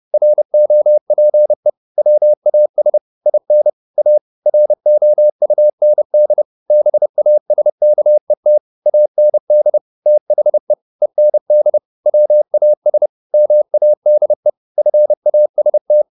Morse code, heard as a keyed tone, about 30 wpm, 600 Hz. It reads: ROPE WAS IN A ROUND BASKET AND THE END WAS MADE FAST